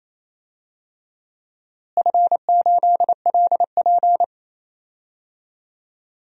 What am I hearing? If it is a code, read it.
Morse code, heard as a keyed tone, about 28 wpm, 700 Hz. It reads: F8LP
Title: DXCC-100 Call Signs - 4 Characters - Repeated 2x - 28wpm